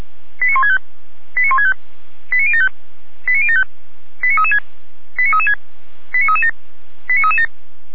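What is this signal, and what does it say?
Five-tone selective call (ZVEI), heard as quick runs of five tones, tones of about 70 ms. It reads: 89165 89165 89075 89075 89336 89336 89337 89337
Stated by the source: ZVEI five-tone selective call received on SDR.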